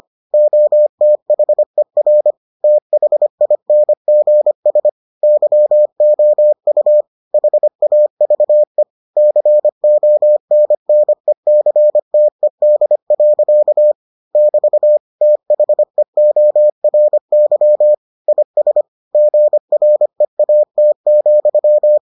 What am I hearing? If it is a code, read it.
Morse code, heard as a keyed tone, about 25 wpm, 600 Hz. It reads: OTHER THINGS YOU HAVE CONNECTED. <BT> THEORY IS GREAT,